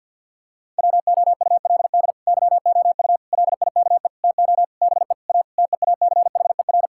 Morse code, heard as a keyed tone, about 50 wpm, 700 Hz. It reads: WOULD YOU LIKE TO BE A TEACHER
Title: Sentences from Top 1000 Words - Review Part 6 - 50wpm